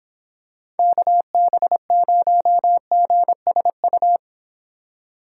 Morse code, heard as a keyed tone, about 26 wpm, 700 Hz. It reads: KB0GSU